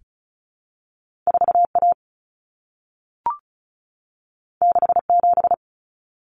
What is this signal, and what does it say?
Morse code, heard as a keyed tone, about 36 wpm, 700 Hz. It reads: 4A 67
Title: Common Cut Numbers - 2-Digit 35wpm